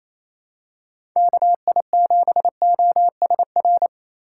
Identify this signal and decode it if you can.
Morse code, heard as a keyed tone, about 28 wpm, 700 Hz. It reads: KI7OSR
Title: USA Call Signs - 6 Characters - 3x Character Spacing - 28wpm